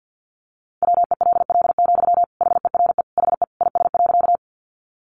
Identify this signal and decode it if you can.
Morse code, heard as a keyed tone, about 50 wpm, 700 Hz. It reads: WELL, HERE HE IS.
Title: Sentences from Top 200 Words 50wpm